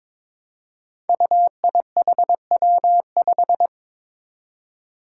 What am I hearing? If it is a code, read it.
Morse code, heard as a keyed tone, about 22 wpm, 700 Hz. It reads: UIHW5
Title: Five Challenging Letter-Number Combinations 22wpm